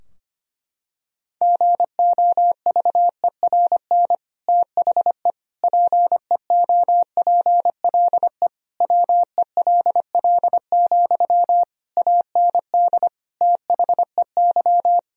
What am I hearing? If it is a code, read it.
Morse code, heard as a keyed tone, about 25 wpm, 700 Hz. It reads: GOVERN THE PEOPLE WELL, AND THEY
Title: Sentences from Top 1000 Words - Review Part 2 - 25wpm